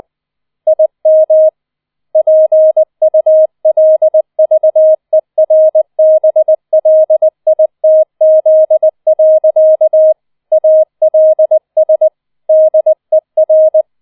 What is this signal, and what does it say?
Morse code, heard as a keyed tone, about 20 wpm, 630 Hz. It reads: IM PULVERBLITZ. ALS DER